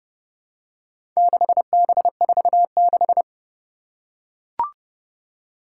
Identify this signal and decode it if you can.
Morse code, heard as a keyed tone, about 29 wpm, 700 Hz. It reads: 6B46
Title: Four Challenging Letter-Number Combinations 30wpm